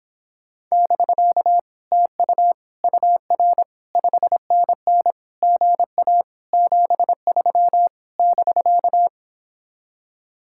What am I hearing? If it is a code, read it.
Morse code, heard as a keyed tone, about 29 wpm, 700 Hz. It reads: <BK> TU UR 5NN GA 73 <BK>